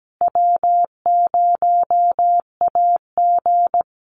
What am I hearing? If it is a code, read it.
Morse code, heard as a keyed tone, about 17 wpm, 700 Hz. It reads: W0AG
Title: Easy Call Signs Repeated 3x - 17wpm